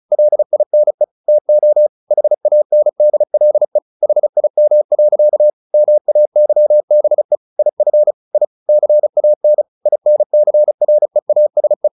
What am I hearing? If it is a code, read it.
Morse code, heard as a keyed tone, about 34 wpm, 600 Hz. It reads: LINE TO HANDLE HIM. MAYBE IF I CAN INCREASE
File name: the old man and the sea / the old man and the sea 1 35wpm